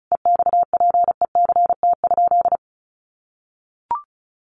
Morse code, heard as a keyed tone, about 34 wpm, 700 Hz. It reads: EXPECT?
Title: Sentences from Top 1100 Words 35wpm